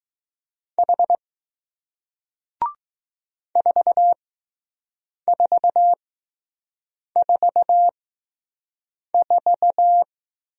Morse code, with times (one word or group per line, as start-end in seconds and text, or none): 0.8-1.2 H
3.6-10.0 4444